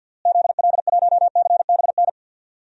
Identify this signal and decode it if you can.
Morse code, heard as a keyed tone, about 52 wpm, 700 Hz. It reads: ZL1CBN
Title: DXCC-100 Call Signs - 6 Characters - Repeated 3x - 50wpm